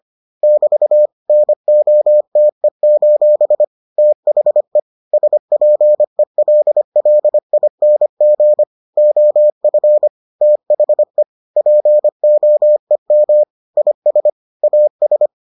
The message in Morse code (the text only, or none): <BT> NOTE: THE SPELLING OF THE POEM IS AS